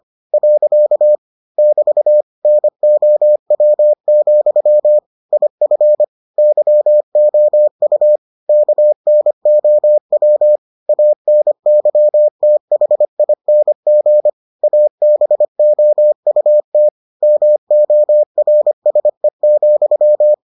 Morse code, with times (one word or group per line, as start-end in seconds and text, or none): 0.3-1.1 .
1.6-2.2 <BT>
2.5-5.0 NOW,
5.3-6.0 IF
6.4-8.2 YOU
8.5-10.6 KNOW
10.9-14.3 ANYTHING
14.6-16.9 ABOUT
17.2-20.3 MORSE,